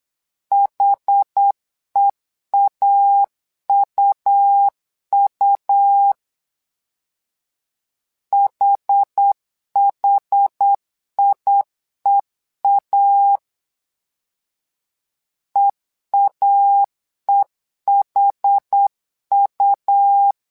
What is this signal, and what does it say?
Morse code, heard as a keyed tone, about 8 wpm, 800 Hz. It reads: HEAUU HHIEA EAEHU
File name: LEZIONI / 2 UHDN / 2 UH 8wpm 2129